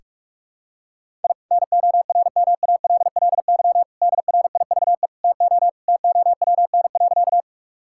Morse code, heard as a keyed tone, about 46 wpm, 700 Hz. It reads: I NORMALLY DRIVE TO TOWN.
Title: Sentences from Top 1000 Words - Review Part 6 - 45wpm